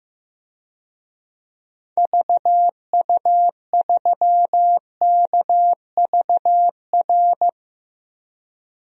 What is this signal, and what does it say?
Morse code, heard as a keyed tone, about 15 wpm, 700 Hz. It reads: VU3KVR
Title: DXCC-100 Call Signs - 6 Characters - Repeated 3x - 15wpm